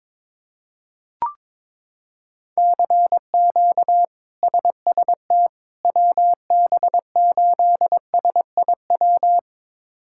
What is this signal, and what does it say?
Morse code, heard as a keyed tone, about 21 wpm, 700 Hz. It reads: CQ SST WB8SIW
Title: K1USN SST QSO Practice - 22wpm